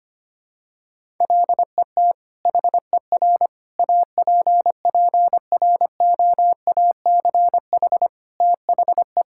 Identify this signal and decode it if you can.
Morse code, heard as a keyed tone, about 25 wpm, 700 Hz. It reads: LET HER APPROACH THE
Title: Sentences from Top 1000 Words - Review Part 6 - 25wpm